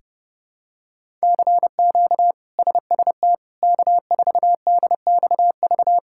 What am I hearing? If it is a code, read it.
Morse code, heard as a keyed tone, about 30 wpm, 700 Hz. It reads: CQ SST K4DXV
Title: K1USN SST QSO Practice - 30wpm